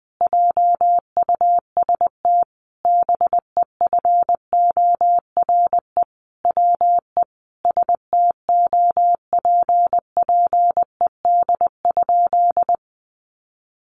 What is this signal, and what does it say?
Morse code, heard as a keyed tone, about 20 wpm, 700 Hz. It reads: JUST BEFORE WE STOPPED?
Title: A Sound Of Thunder 20wpm